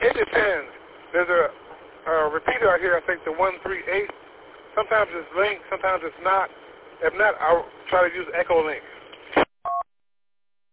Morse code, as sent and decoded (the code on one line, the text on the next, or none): none